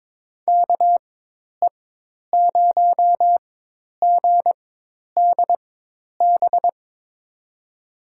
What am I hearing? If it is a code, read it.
Morse code, heard as a keyed tone, about 22 wpm, spaced as for 10 wpm, 700 Hz. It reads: KE0GDB
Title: USA Call Signs - 6 Characters - 4x Character Spacing - 22wpm